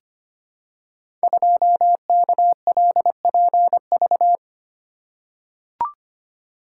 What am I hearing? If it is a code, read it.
Morse code, heard as a keyed tone, about 25 wpm, 700 Hz. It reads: 2KLPV
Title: Code Groups - 5 Letter:Numbers - 25wpm